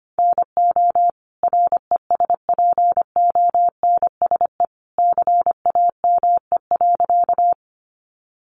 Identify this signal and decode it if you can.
Morse code, heard as a keyed tone, about 25 wpm, 700 Hz. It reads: NO RESPONSE CAME.
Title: Sentences from Top 1400 Words 25wpm